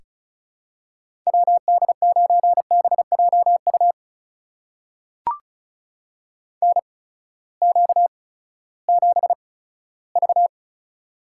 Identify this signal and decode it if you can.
Morse code, heard as a keyed tone, about 36 wpm, spaced as for 18 wpm, 700 Hz. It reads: WD9BJU NQ7V
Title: USA Call Signs - 8x Character Spacing - 35wpm